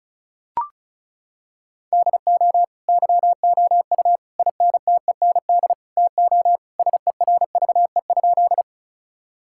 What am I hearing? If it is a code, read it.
Morse code, heard as a keyed tone, about 36 wpm, 700 Hz. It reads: DO YOU INTEND TO SERVE?
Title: Sentences from Top 1600 Words 35wpm